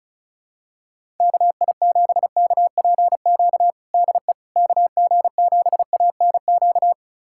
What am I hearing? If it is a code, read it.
Morse code, heard as a keyed tone, about 34 wpm, 700 Hz. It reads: KI7KPQ DE KG7ANQ
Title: USA Callsign DE Callsign - 6 Characters - 35wpm